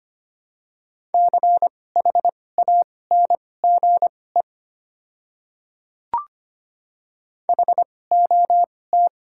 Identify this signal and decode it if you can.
Morse code, heard as a keyed tone, about 25 wpm, spaced as for 18 wpm, 700 Hz. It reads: CHANGE HOT
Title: 2x Character Spacing - Top 200 Words - 25wpm